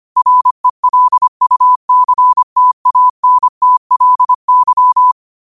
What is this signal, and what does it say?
Morse code, heard as a keyed tone, about 25 wpm, 1000 Hz. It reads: RELUCTANTLY
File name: qotdfiles / 25wpm